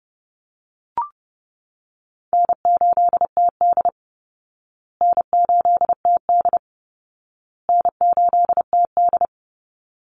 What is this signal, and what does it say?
Morse code, heard as a keyed tone, about 30 wpm, 700 Hz. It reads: N8TD N8TD N8TD
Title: Easy Call Signs Repeated 3x - 30wpm